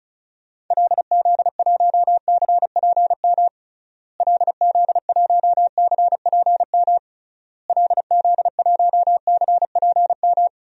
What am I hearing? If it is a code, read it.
Morse code, heard as a keyed tone, about 34 wpm, 700 Hz. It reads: LZ1CPM LZ1CPM LZ1CPM
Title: DXCC-100 Call Signs - 6 Characters - Repeated 3x - 35wpm